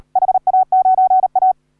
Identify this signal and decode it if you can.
Morse code, heard as a keyed tone, about 38 wpm, 750 Hz. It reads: HA9A
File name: AUDIO2019 / HA3OU CW